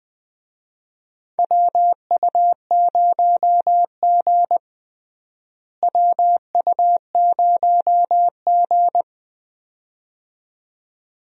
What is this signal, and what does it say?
Morse code, heard as a keyed tone, about 20 wpm, 700 Hz. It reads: WU0G WU0G
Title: USA Call Signs - 4 Characters - Repeated 2x - 20wpm